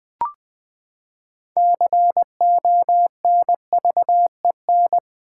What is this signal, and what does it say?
Morse code, heard as a keyed tone, about 20 wpm, 700 Hz. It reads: CONVEN
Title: Build A Word - 6 Letters At A Time - 12-Letter Words - 20wpm